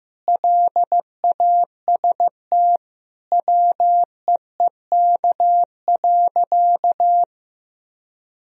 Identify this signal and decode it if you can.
Morse code, heard as a keyed tone, about 15 wpm, 700 Hz. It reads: LAST WEEK.
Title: Sentences from Top 1000 Words - Review Part 2 - 15wpm